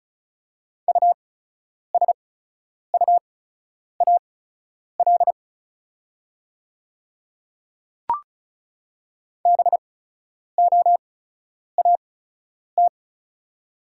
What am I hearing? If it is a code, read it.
Morse code, heard as a keyed tone, about 36 wpm, spaced as for 10 wpm, 700 Hz. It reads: USUAL BOAT
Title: 8x Character Spacing - Top 500 Words - 35wpm